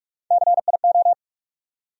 Morse code, heard as a keyed tone, about 47 wpm, 700 Hz. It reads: CIO
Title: USA Call Sign Postfixes - 3 Letters 45wpm